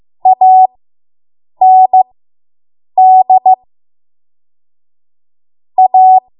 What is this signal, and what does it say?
Morse code, heard as a keyed tone, about 15 wpm, spaced as for 7 wpm, 750 Hz. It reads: AND A